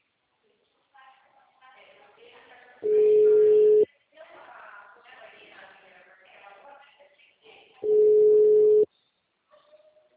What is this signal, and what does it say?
Ringback tone (Europe, 425 Hz), with speech.
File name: out- -2004- - - .48497